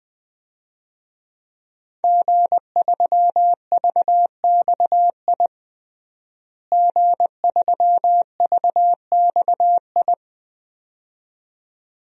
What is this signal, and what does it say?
Morse code, heard as a keyed tone, about 20 wpm, 700 Hz. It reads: G3VXI G3VXI
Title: DXCC-100 Call Signs - 5 Characters - Repeated 2x - 20wpm